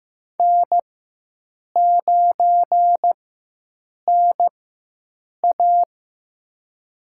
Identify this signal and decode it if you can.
Morse code, heard as a keyed tone, about 15 wpm, spaced as for 7 wpm, 700 Hz. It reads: N9NA
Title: USA Call Signs - 4 Characters - 4x Character Spacing - 15wpm